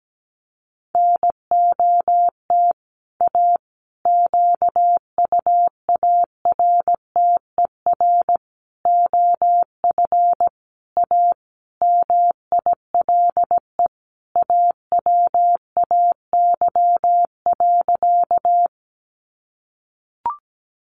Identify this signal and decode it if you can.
Morse code, heard as a keyed tone, about 17 wpm, 700 Hz. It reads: NOT A QUARTER OF A MILE AWAY.
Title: Sentences from Top 1300 Words 17wpm